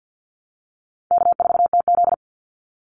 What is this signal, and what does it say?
Morse code, heard as a keyed tone, about 49 wpm, 700 Hz. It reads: K4TZ
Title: Call Signs Repeated 3x - 50wpm